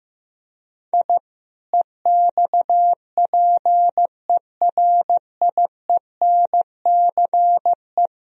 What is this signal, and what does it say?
Morse code, heard as a keyed tone, about 15 wpm, 700 Hz. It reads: I EXPERIENCE
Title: Sentences from Top 1000 Words - Review Part 3 - 15wpm